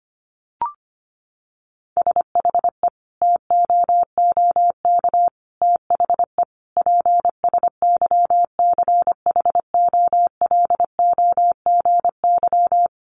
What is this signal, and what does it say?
Morse code, heard as a keyed tone, about 25 wpm, 700 Hz. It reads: SHE TOOK THE PSYCHOLOGY
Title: Sentences from Top 1400 Words 25wpm